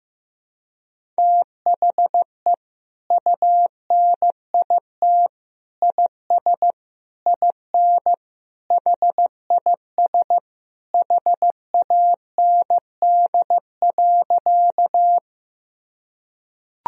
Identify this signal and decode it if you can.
Morse code, heard as a keyed tone, about 15 wpm, 700 Hz. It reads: THE UNIT IS IN HIS HAND.
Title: Sentences from Top 1000 Words - Review Part 2 - 15wpm